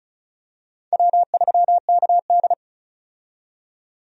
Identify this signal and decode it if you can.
Morse code, heard as a keyed tone, about 34 wpm, 700 Hz. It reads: W3KD
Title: USA Call Signs - 4 Characters - 35wpm